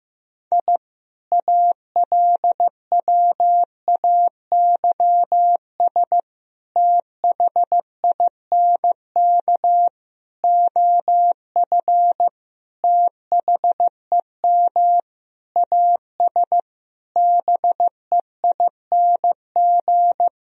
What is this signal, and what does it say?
Morse code, heard as a keyed tone, about 15 wpm, 700 Hz. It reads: I ALWAYS THINK OF THEM AS BEING
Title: Sentences from Top 1000 Words - Review Part 3 - 15wpm